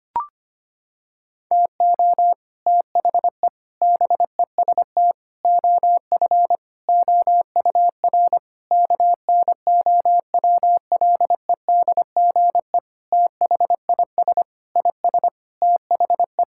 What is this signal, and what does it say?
Morse code, heard as a keyed tone, about 25 wpm, 700 Hz. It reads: TO THE BEST OF OUR KNOWLEDGE THIS IS THE
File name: practice / 25wpm / Sentences with 9-Word Phrases - 25wpm